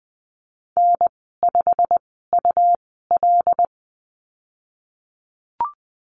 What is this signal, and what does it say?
Morse code, heard as a keyed tone, about 20 wpm, spaced as for 14 wpm, 700 Hz. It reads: N5UL
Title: 161 Lesson 26g - Letter g - Call Signs 20wpm